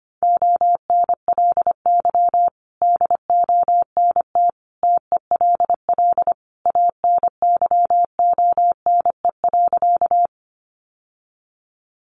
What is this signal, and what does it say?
Morse code, heard as a keyed tone, about 25 wpm, 700 Hz. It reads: ONLY DONT TELL ANYONE.
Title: Sentences from Top 900 Words 25wpm